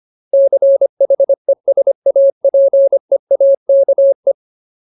Morse code, heard as a keyed tone, about 25 wpm, 550 Hz. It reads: CHESAPEAKE
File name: Ultimate Word List / 25WPM Ultimate